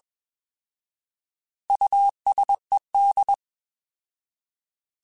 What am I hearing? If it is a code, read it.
Morse code, heard as a keyed tone, about 20 wpm, 800 Hz. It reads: USED
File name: Alphabetical / Alphabetical mp3s / Us @20